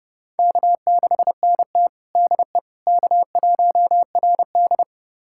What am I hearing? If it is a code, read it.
Morse code, heard as a keyed tone, about 32 wpm, 700 Hz. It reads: K6NT DE K1RD
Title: USA Callsign DE Callsign - 4 Characters - 30wpm